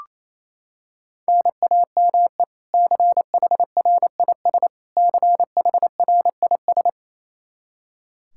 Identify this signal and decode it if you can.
Morse code, heard as a keyed tone, about 27 wpm, 700 Hz. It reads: NAME CHRIS CHRIS
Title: Practice QSO - Exchange 1 - Beginning - 28wpm